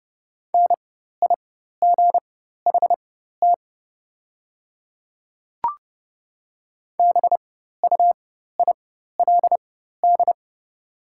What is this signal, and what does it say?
Morse code, heard as a keyed tone, about 30 wpm, spaced as for 14 wpm, 700 Hz. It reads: NIGHT BUILD
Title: 4x Character Spacing - Top 500 Words - 30wpm